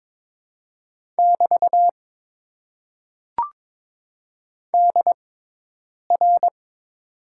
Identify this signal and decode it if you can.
Morse code, heard as a keyed tone, about 22 wpm, spaced as for 8 wpm, 700 Hz. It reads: <BT> DR
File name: Top 90 QSO Elements - 6x Character Spacing - 22wpm